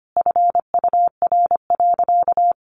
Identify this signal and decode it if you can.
Morse code, heard as a keyed tone, about 25 wpm, 700 Hz. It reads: FUR.
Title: Easy Phrases 25wpm